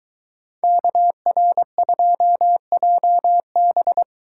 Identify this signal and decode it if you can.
Morse code, heard as a keyed tone, about 22 wpm, 700 Hz. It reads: KR2JB